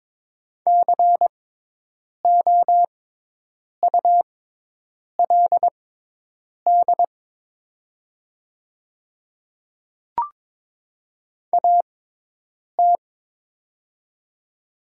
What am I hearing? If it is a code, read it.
Morse code, heard as a keyed tone, about 22 wpm, spaced as for 8 wpm, 700 Hz. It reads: COULD AT
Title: 6x Character Spacing - Top 500 Words - 22wpm